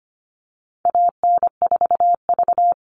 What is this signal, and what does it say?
Morse code, heard as a keyed tone, about 25 wpm, 700 Hz. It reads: AN4V
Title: All Cut Numbers - 4-Digit 25wpm